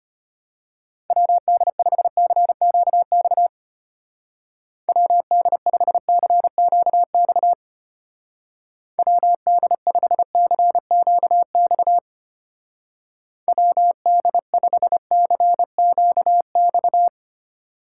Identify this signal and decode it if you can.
Morse code, heard as a keyed tone, about 32 wpm, 700 Hz. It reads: WD5CQX WD5CQX WD5CQX WD5CQX